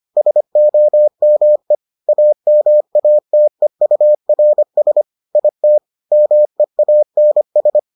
Morse code, heard as a keyed tone, about 25 wpm, 600 Hz. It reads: SOME AMATEURS IT MEANS